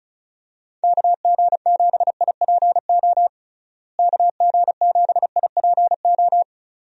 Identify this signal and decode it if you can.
Morse code, heard as a keyed tone, about 36 wpm, 700 Hz. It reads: KG7IPO KG7IPO
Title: USA Call Signs - 6 Characters - Repeated 2x - 35wpm